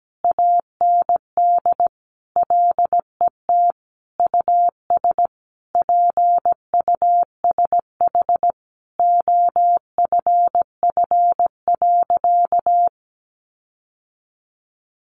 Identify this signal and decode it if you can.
Morse code, heard as a keyed tone, about 17 wpm, 700 Hz. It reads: AND LET US PUSH OFF.
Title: Sentences from Top 1300 Words 17wpm